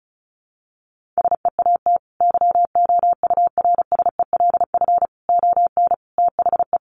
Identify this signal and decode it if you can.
Morse code, heard as a keyed tone, about 34 wpm, 700 Hz. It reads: SEAT YOURSELF ON THE
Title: Sentences from Top 1500 Words 35wpm